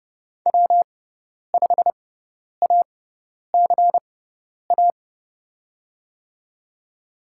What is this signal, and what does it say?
Morse code, heard as a keyed tone, about 32 wpm, spaced as for 10 wpm, 700 Hz. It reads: W5ACA
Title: USA Call Signs - 5 Characters - 6x Character Spacing - 30wpm